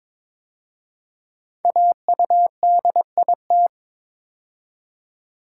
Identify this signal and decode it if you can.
Morse code, heard as a keyed tone, about 22 wpm, 700 Hz. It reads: AUDIT
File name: 5 Letter Words - First 3 Letters Given - 22wpm